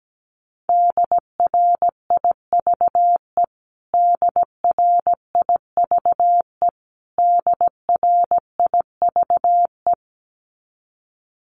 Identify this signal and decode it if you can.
Morse code, heard as a keyed tone, about 17 wpm, 700 Hz. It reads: DRIVE DRIVE DRIVE
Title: Top 500 Words Repeated 17wpm